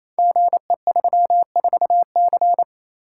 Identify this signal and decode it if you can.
Morse code, heard as a keyed tone, about 29 wpm, 690 Hz. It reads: GE34C
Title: Code Groups - 5 Letter:Numbers - 28wpm